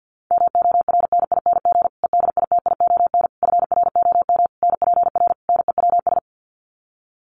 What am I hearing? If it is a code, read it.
Morse code, heard as a keyed tone, about 50 wpm, 700 Hz. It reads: MORNING EDITION FROM NPR NEWS
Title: Sets of 5 Words Encore 50wpm